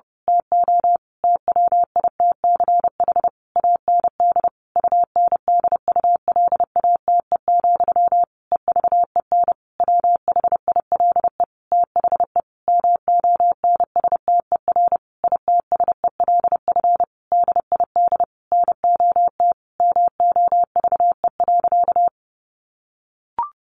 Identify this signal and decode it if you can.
Morse code, heard as a keyed tone, about 30 wpm, 700 Hz. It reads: TO TWITCH AND UNDULATE, EVEN WHILE THE MONSTER ITSELF DID NOT MOVE.
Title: A Sound Of Thunder 30wpm